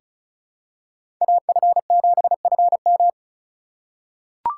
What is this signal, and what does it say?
Morse code, heard as a keyed tone, about 36 wpm, 700 Hz. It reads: AF7FM
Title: USA Call Signs - 5 Characters - 2x Character Spacing - 35wpm